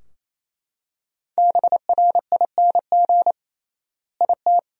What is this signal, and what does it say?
Morse code, heard as a keyed tone, about 28 wpm, 700 Hz. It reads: BRING IT
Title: 3x Word-Spacing - Sentences from Top 500 Words - 28wpm